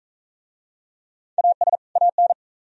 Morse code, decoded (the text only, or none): AS AN